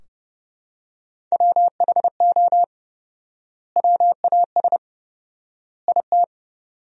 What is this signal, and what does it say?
Morse code, heard as a keyed tone, about 30 wpm, 700 Hz. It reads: WHO WAS IT
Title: 4x Word-Spacing - Sentences from Top 400 Words - 30wpm